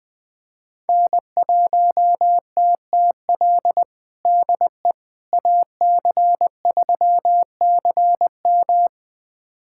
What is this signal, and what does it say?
Morse code, heard as a keyed tone, about 21 wpm, 700 Hz. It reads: N1TTL DE AC3CM
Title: USA Callsign DE Callsign - 5 Characters - 20wpm